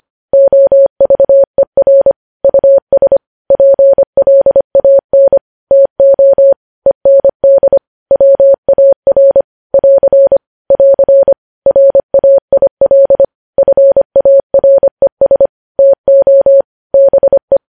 Morse code, heard as a keyed tone, about 25 wpm, 550 Hz. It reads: OVER US PLAN TO END WAR <AR> <AR> RAIL FARES TO BE
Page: News Headlines in Morse Code at 25 WPM